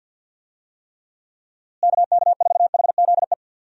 Morse code, heard as a keyed tone, about 45 wpm, 700 Hz. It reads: KK4HZE
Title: USA Call Signs - 6 Characters - Repeated 2x - 50wpm